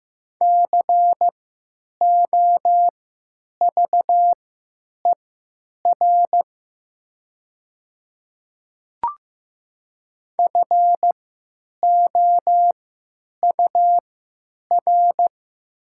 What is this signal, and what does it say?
Morse code, heard as a keyed tone, about 15 wpm, spaced as for 9 wpm, 700 Hz. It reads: COVER FOUR
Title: 3x Character Spacing - Top 300 Words - 15wpm